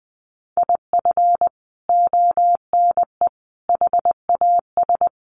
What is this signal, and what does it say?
Morse code, heard as a keyed tone, about 20 wpm, 700 Hz. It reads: IF ONE HAS
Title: Sentences from Top 1400 Words 20wpm